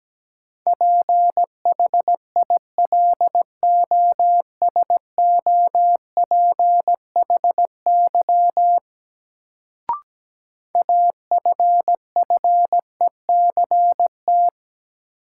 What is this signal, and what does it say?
Morse code, heard as a keyed tone, about 17 wpm, 700 Hz. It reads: PHILOSOPHY AFFECT
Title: Sentences from Top 900 Words - New Word Drill - 17wpm